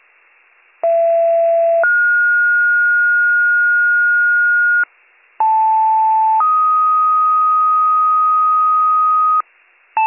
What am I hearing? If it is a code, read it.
Two-tone page, heard as a one-second tone, then a three-second tone, about 670 then 1430 Hz, about 880 then 1250 Hz, with noise. It is heard from 0.8 to 9.4 s.